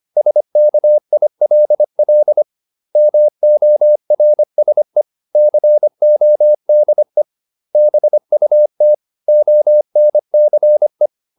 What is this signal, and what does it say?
Morse code, heard as a keyed tone, about 25 wpm, 600 Hz. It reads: SKILL MORSE CODE BUT ONCE